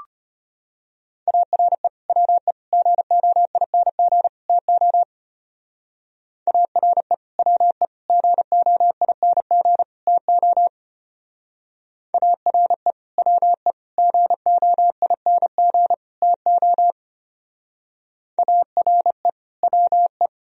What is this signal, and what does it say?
Morse code, heard as a keyed tone, about 33 wpm, 700 Hz. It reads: ARE WE GOING TO ARE WE GOING TO ARE WE GOING TO ARE WE